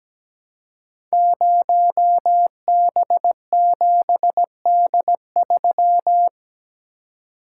Morse code, heard as a keyed tone, about 17 wpm, 700 Hz. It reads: 0B7D3
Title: Code Groups - 5 Letter:Numbers - 17wpm